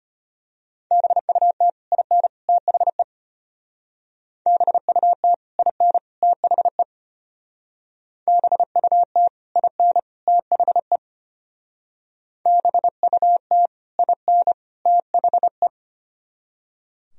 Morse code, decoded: BUT IN THE BUT IN THE BUT IN THE BUT IN THE

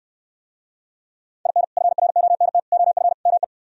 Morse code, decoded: IN FRONT OF ME

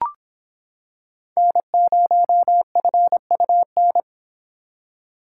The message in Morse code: N0FUN

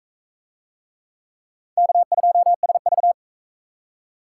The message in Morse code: K2SV

